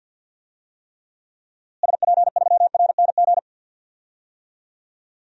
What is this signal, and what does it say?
Morse code, heard as a keyed tone, about 46 wpm, 700 Hz. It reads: SP3RNG
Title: DXCC-100 Call Signs - 6 Characters - Repeated 3x - 50wpm